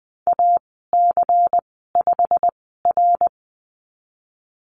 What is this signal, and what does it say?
Morse code, heard as a keyed tone, about 20 wpm, spaced as for 14 wpm, 700 Hz. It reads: AC5R
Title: 161 Lesson 26g - Letter g - Call Signs 20wpm